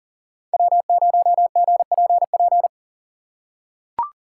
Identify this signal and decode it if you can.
Morse code, heard as a keyed tone, about 38 wpm, 700 Hz. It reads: W0GPP
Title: USA Call Signs - 5 Characters - 4x Character Spacing - 40wpm